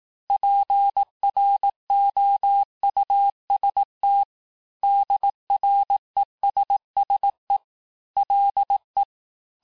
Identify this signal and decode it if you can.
Morse code, heard as a keyed tone, about 18 wpm, 800 Hz. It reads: PROUST DRESSE LE
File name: cours-du-lundi files / cours-1594--teste-clair-vitesse-17mpm-puis-22mpm